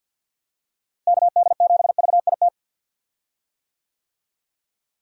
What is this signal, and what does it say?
Morse code, heard as a keyed tone, about 49 wpm, 700 Hz. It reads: KD7VIT
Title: USA Call Signs - 6 Characters - Repeated 2x - 50wpm